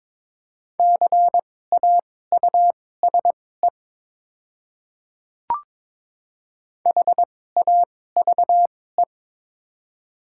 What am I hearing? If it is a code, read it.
Morse code, heard as a keyed tone, about 21 wpm, spaced as for 16 wpm, 690 Hz. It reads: CAUSE HAVE
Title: 2x Character Spacing - Top 500 Words - 22wpm